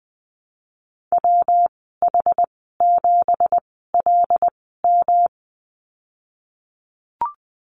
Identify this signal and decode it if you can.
Morse code, heard as a keyed tone, about 20 wpm, spaced as for 15 wpm, 700 Hz. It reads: WH7LM
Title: 193 Lesson 31g - Letter v - Call Signs 20wpm